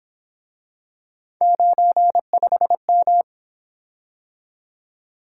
Morse code, heard as a keyed tone, about 27 wpm, 700 Hz. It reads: 95M
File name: Speed-Racing - Three Letter-Number Combinations - 22wpm